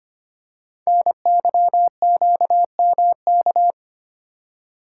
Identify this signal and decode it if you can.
Morse code, heard as a keyed tone, about 25 wpm, 700 Hz. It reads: NYQMK
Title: Code Groups - 5 Letters - 25wpm